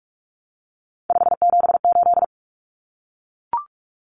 Morse code, heard as a keyed tone, about 44 wpm, 700 Hz. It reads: 578
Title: Numbers - 3-Digit 45wpm